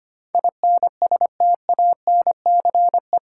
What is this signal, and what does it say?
Morse code, heard as a keyed tone, about 25 wpm, 700 Hz. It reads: INSTANCE
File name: Eight Letter Words - 25wpm